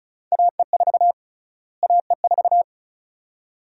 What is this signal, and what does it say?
Morse code, heard as a keyed tone, about 35 wpm, 700 Hz. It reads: AE4 AE4
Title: USA Call Sign Prefixes - Repeated 2x - 35wpm